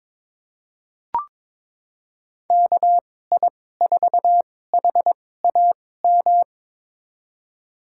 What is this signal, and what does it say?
Morse code, heard as a keyed tone, about 22 wpm, spaced as for 16 wpm, 700 Hz. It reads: KI4HAM
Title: USA Call Signs - 6 Characters - 2x Character Spacing - 22wpm